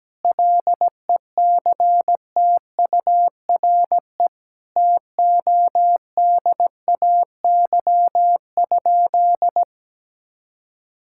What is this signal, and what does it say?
Morse code, heard as a keyed tone, about 17 wpm, 700 Hz. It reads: LECTURE TODAY?
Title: Sentences from Top 1600 Words 17wpm